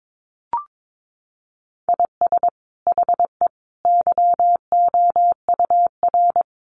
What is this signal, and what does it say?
Morse code, heard as a keyed tone, about 22 wpm, 700 Hz. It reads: IS HE YOUR
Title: Sentences from Top 1000 Words 22wpm